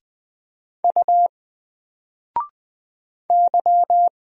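Morse code, heard as a keyed tone, about 20 wpm, 700 Hz. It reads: U Y
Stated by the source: Single letter and number Morse drill at 20 wpm, from N4LCD.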